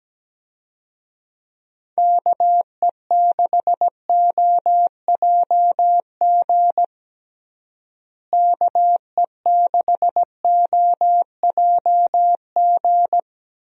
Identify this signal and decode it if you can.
Morse code, heard as a keyed tone, about 17 wpm, 700 Hz. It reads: KE6OJG KE6OJG
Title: USA Call Signs - 6 Characters - Repeated 3x - 17wpm